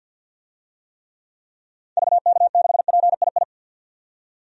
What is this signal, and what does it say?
Morse code, heard as a keyed tone, about 47 wpm, 700 Hz. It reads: VK6PII
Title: DXCC-100 Call Signs - 6 Characters - 50wpm